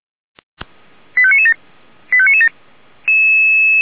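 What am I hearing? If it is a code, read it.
Five-tone selective call (ZVEI), heard as quick runs of five tones, tones of about 70 ms. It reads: 85997 85997
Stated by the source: ZVEI five-tone call received on SDR.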